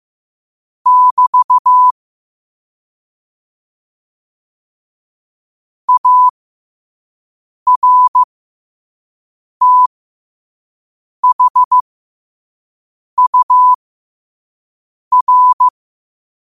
Morse code, heard as a keyed tone, about 15 wpm, spaced as for 5 wpm, 1000 Hz. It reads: <BT> ARTHUR
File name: qotdfiles / 5wpm